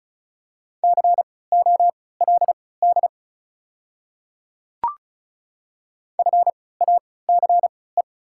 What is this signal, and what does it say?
Morse code, heard as a keyed tone, about 36 wpm, spaced as for 20 wpm, 690 Hz. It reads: COLD FACE
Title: 3x Character Spacing - Top 500 Words - 35wpm